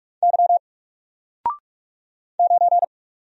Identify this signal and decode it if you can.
Morse code, heard as a keyed tone, about 44 wpm, 700 Hz. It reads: Y 9
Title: Single Character (Letter:Number:Punctuation:Prosign) 45wpm